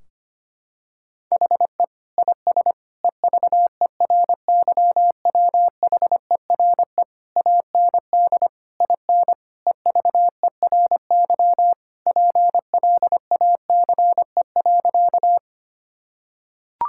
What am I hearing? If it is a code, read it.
Morse code, heard as a keyed tone, about 25 wpm, 700 Hz. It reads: HE IS EVERYWHERE AND IN EVERY PLACE.
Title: Sentences from Top 1600 Words 25wpm